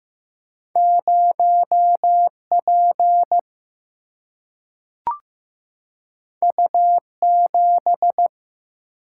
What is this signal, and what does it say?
Morse code, heard as a keyed tone, about 15 wpm, 700 Hz. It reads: 0 P U 7
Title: Single-Letter Single-Number 15wpm